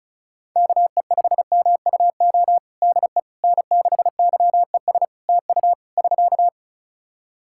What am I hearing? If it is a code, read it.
Morse code, heard as a keyed tone, about 36 wpm, 700 Hz. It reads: KE5MUO DE N6YES TU <SK>
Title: K1USN SST QSO Practice 35wpm